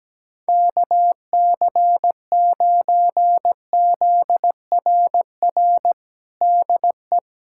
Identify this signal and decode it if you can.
Morse code, heard as a keyed tone, about 17 wpm, 700 Hz. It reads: KC9ZRR DE
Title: USA Callsign DE Callsign 17wpm